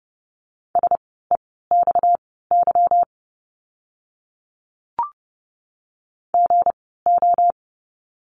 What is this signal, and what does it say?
Morse code, heard as a keyed tone, about 30 wpm, spaced as for 17 wpm, 700 Hz. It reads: SEXY GO
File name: course / 30:10wpm / 230 Lesson 37f - Letter x - Words 30wpm